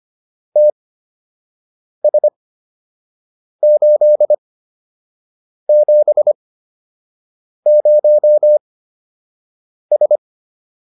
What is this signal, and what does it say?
Morse code, heard as a keyed tone, about 25 wpm, spaced as for 6 wpm, 600 Hz. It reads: TS870S